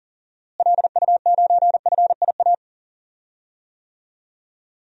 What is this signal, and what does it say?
Morse code, heard as a keyed tone, about 39 wpm, 700 Hz. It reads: LU9FIA
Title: DXCC-100 Call Signs - 6 Characters - Repeated 3x - 40wpm